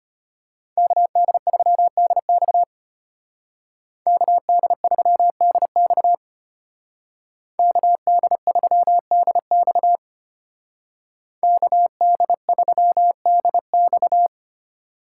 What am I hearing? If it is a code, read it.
Morse code, heard as a keyed tone, about 32 wpm, 700 Hz. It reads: KD3DX KD3DX KD3DX KD3DX